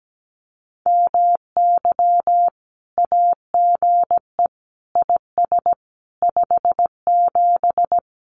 Morse code, heard as a keyed tone, about 17 wpm, 700 Hz. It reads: MY AGE IS 57
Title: Common QSO Phrases Encore 17wpm